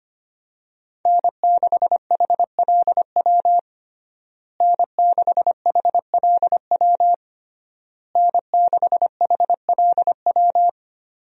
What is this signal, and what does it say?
Morse code, heard as a keyed tone, about 25 wpm, 700 Hz. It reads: N6HLW N6HLW N6HLW
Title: USA Call Signs - 5 Characters - Repeated 3x - 25wpm